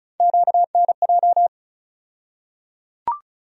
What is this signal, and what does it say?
Morse code, heard as a keyed tone, about 34 wpm, 700 Hz. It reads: QNJ
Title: USA Call Sign Postfixes - 3 Letters 35wpm